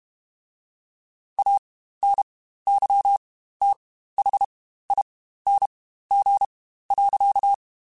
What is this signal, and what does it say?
Morse code, heard as a keyed tone, about 32 wpm, spaced as for 15 wpm, 800 Hz. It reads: ANYTHING.
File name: Quotes Spaced / Quotes Spaced-mp3s / StandFor sp @30